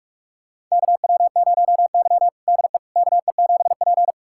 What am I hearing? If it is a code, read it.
Morse code, heard as a keyed tone, about 46 wpm, 700 Hz. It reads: KW0Y DE KE7P
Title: USA Callsign DE Callsign - 4 Characters - 45wpm